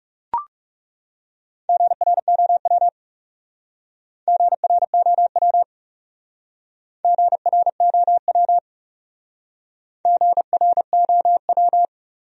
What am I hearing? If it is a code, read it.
Morse code, heard as a keyed tone, about 36 wpm, 700 Hz. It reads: GROW GROW GROW GROW
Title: Speed-Racing Top 300 Words 30wpm